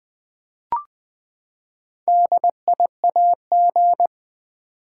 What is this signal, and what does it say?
Morse code, heard as a keyed tone, about 20 wpm, 700 Hz. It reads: DIAG
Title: Build A Word - 4 Letters At A Time - 8-Letter Words - 20wpm